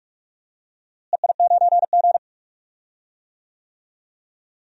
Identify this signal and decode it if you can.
Morse code, heard as a keyed tone, about 44 wpm, 700 Hz. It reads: EI9G
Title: DXCC-100 Call Signs - 4 Characters - Repeated 2x - 45wpm